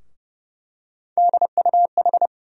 Morse code, heard as a keyed tone, about 32 wpm, 700 Hz. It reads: DUH
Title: USA Call Sign Postfixes - 3 Letters 30wpm